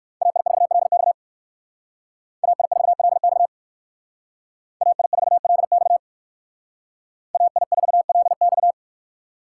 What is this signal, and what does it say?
Morse code, heard as a keyed tone, about 58 wpm, 700 Hz. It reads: AI4LX AI4LX AI4LX AI4LX